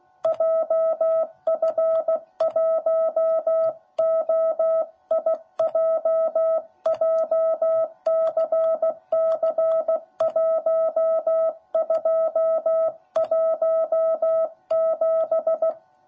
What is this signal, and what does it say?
Morse code, heard as a keyed tone, about 15 wpm, 660 Hz. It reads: JF1OIJ JCC1217
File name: jh8cbh / 2021cw / cw10b